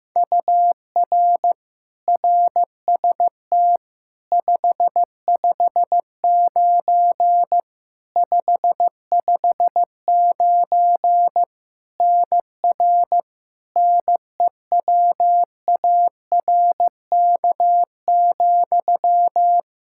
Morse code, heard as a keyed tone, about 15 wpm, 700 Hz. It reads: UR RST 559 559 NR NEWARK,
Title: K1USN SST QSO Practice 15wpm